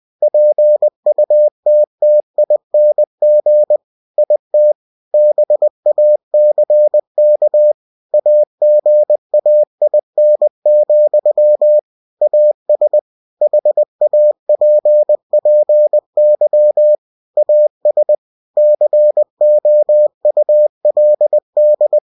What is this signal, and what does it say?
Morse code, heard as a keyed tone, about 20 wpm, 600 Hz. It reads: PUTTING IT BACK AGAIN, AS HAPPY AS COULD
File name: cw-books / winnie-the-pooh / winnie-20wpm-page-0058